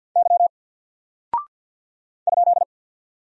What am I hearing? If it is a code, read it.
Morse code, heard as a keyed tone, about 48 wpm, 700 Hz. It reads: Y ?
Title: Single Character (Letter:Number:Punctuation:Prosign) Alt 50wpm